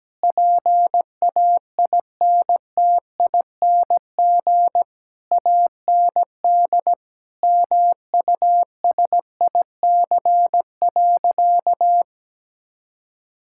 Morse code, heard as a keyed tone, about 17 wpm, 700 Hz. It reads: PAINTING AND MUSIC.
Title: Sentences from Top 1000 Words - Review Part 6 - 17wpm